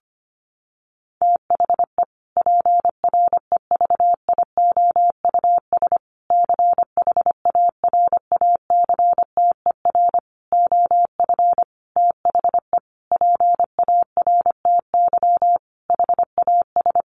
Morse code, heard as a keyed tone, about 25 wpm, 700 Hz. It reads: THE PREVIOUS CHARACTER OF THE PARTY HAS
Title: Sentences from Top 1500 Words 25wpm